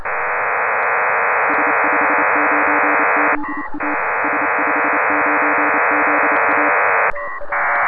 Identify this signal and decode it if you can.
Morse code, heard as a keyed tone, about 28 wpm, 300 Hz, with noise. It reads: S59ZA S59ZA
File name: Pile-up with PACTOR3